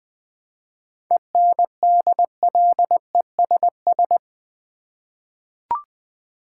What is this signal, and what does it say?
Morse code, heard as a keyed tone, about 20 wpm, 700 Hz. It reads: ENDLESS
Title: 7 Letter Words - First 3 Letters Given - 20wpm